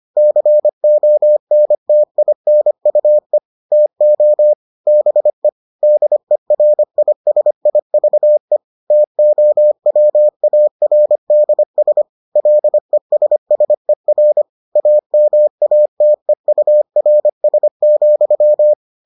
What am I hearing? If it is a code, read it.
Morse code, heard as a keyed tone, about 25 wpm, 600 Hz. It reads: CONTINUE TO BE DERISIVE TOWARDS LESSER AMATEURS,